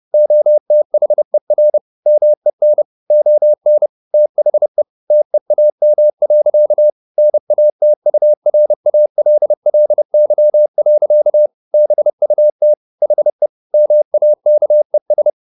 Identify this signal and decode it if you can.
Morse code, heard as a keyed tone, about 30 wpm, 600 Hz. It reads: OTHER MEN ON THE TEAM. NATURALLY. BUT HE MAKES
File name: the old man and the sea / the old man and the sea 1 30wpm